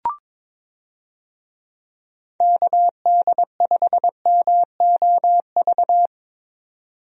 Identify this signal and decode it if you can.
Morse code, heard as a keyed tone, about 21 wpm, 700 Hz. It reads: KD5MOV
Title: USA Call Signs - 6 Characters - 22wpm